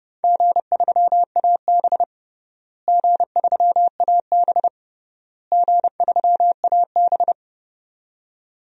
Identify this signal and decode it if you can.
Morse code, heard as a keyed tone, about 30 wpm, 700 Hz. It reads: G3AB G3AB G3AB
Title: DXCC-100 Call Signs - 4 Characters - Repeated 3x - 30wpm